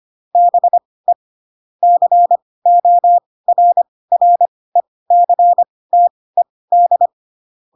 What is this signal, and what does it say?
Morse code, heard as a keyed tone, about 25 wpm, spaced as for 18 wpm, 700 Hz. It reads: BE CORRECTED
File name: Pinocchio / Pinocchio ch4 12 25 ik1hge